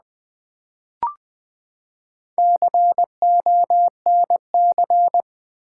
Morse code, heard as a keyed tone, about 19 wpm, 700 Hz. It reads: CONC
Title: Build A Word - 4 Letters At A Time - 10-Letter Words - 20wpm